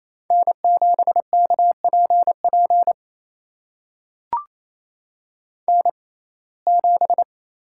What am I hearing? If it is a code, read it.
Morse code, heard as a keyed tone, about 28 wpm, spaced as for 20 wpm, 700 Hz. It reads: N7KPP N7
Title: USA Call Signs - 5 Characters - 6x Character Spacing - 28wpm